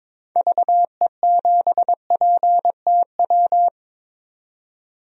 Morse code, heard as a keyed tone, about 22 wpm, 700 Hz. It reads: VE7PTW
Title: DXCC-100 Call Signs - 6 Characters - 22wpm